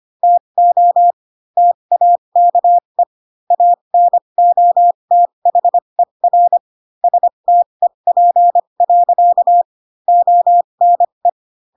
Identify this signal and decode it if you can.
Morse code, heard as a keyed tone, about 25 wpm, spaced as for 22 wpm, 700 Hz. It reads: TO TAKE ANOTHER STEP. ONE
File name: Pinocchio / Pinocchio ch3 18 25 ik1hge